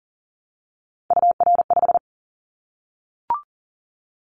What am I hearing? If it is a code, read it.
Morse code, heard as a keyed tone, about 38 wpm, 700 Hz. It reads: UR5
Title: Callsign Prefixes for Top 25 DXCC - 40wpm